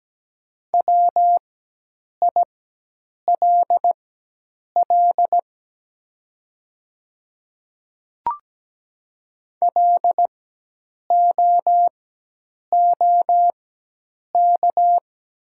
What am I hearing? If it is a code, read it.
Morse code, heard as a keyed tone, about 17 wpm, spaced as for 8 wpm, 700 Hz. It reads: WILL LOOK
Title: 4x Character Spacing - Top 100 Words - 17wpm